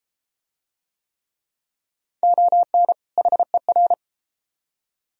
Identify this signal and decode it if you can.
Morse code, heard as a keyed tone, about 32 wpm, 700 Hz. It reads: ON HER